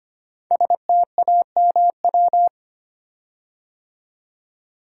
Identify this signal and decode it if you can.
Morse code, heard as a keyed tone, about 25 wpm, 700 Hz. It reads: STAMW
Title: Code Groups - 5 Letters - 25wpm